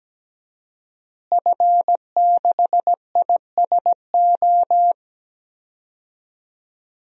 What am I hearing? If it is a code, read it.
Morse code, heard as a keyed tone, about 17 wpm, 700 Hz. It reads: F6ISO
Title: DXCC-100 Call Signs - 5 Characters - Repeated 3x - 17wpm